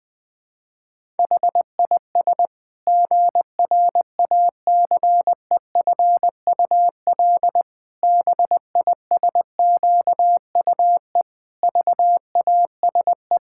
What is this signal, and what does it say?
Morse code, heard as a keyed tone, about 20 wpm, 700 Hz. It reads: HIS GRACEFUL BISQUE VASE
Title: Pangram Challenge v2 - 20wpm